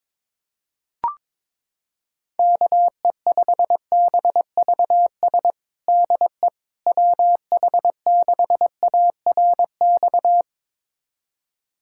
Morse code, heard as a keyed tone, about 22 wpm, 700 Hz. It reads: KE5BVS DE WH6ARX
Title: USA Callsign DE Callsign - 6 Characters - 22wpm